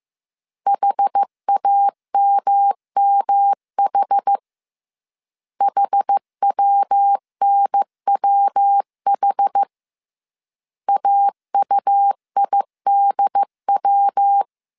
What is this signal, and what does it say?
Morse code, heard as a keyed tone, about 15 wpm, 800 Hz. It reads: HAMMH HWNWH AUIDW